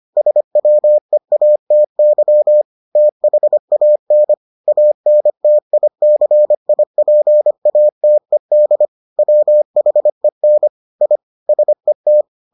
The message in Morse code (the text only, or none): SWEATY THAN ANTICIPATED WHEN I SET